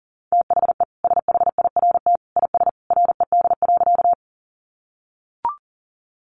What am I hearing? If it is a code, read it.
Morse code, heard as a keyed tone, about 40 wpm, 700 Hz. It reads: THE SHIRT IS RED.
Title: Additional Easy Phrases 40wpm